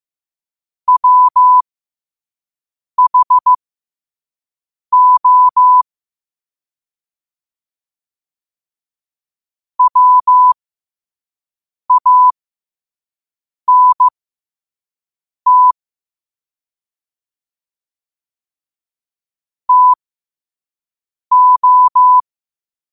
Morse code, heard as a keyed tone, about 15 wpm, spaced as for 5 wpm, 1000 Hz. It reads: WHO WANT TO
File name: qotdfiles / 5wpm